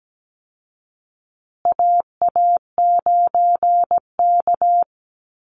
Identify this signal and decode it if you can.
Morse code, heard as a keyed tone, about 17 wpm, 700 Hz. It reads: AA9K
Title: Call Signs Repeated 3x - 17wpm